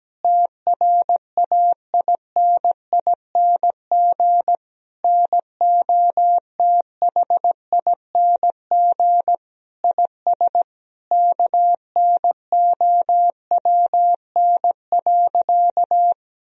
Morse code, heard as a keyed tone, about 17 wpm, 700 Hz. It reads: TRAINING NOTHING IS KNOWN.
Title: Sentences from Top 1000 Words - Review Part 4 - 17wpm